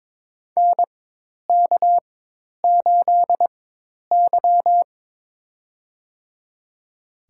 Morse code, heard as a keyed tone, about 21 wpm, spaced as for 10 wpm, 700 Hz. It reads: NK8Y
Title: USA Call Signs - 4 Characters - 4x Character Spacing - 22wpm